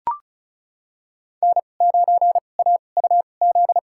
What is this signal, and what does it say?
Morse code, heard as a keyed tone, about 36 wpm, spaced as for 25 wpm, 700 Hz. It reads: N9AUZ
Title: USA Call Signs - 2x Character Spacing - 35wpm